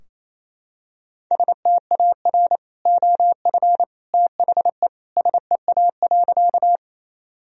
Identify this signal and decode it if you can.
Morse code, heard as a keyed tone, about 27 wpm, 700 Hz. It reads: STAR OF THE SEA.
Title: Sentences from Top 1000 Words - Review Part 2 - 28wpm